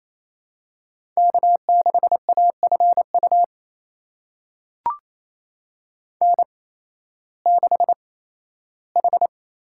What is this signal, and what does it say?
Morse code, heard as a keyed tone, about 27 wpm, spaced as for 15 wpm, 700 Hz. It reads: K6AFU N6H
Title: USA Call Signs - 8x Character Spacing - 28wpm